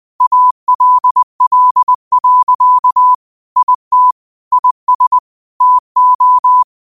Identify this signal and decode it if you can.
Morse code, heard as a keyed tone, about 20 wpm, 1000 Hz. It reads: ALL. IT IS TO
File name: qotdfiles / 20wpm